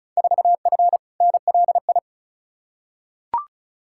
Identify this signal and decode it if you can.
Morse code, heard as a keyed tone, about 36 wpm, 700 Hz. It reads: 4F NLI
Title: Field Day Exchanges Encore - 35wpm